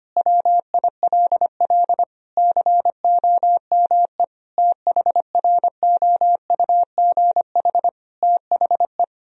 Morse code, heard as a keyed tone, about 25 wpm, 700 Hz. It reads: WILL COME THROUGH THE
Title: Sentences from Top 1000 Words - Review Part 1 - 25wpm